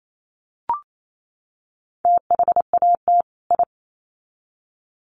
Morse code, heard as a keyed tone, about 28 wpm, 700 Hz. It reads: THAT I
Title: Sets of 2 Words 28wpm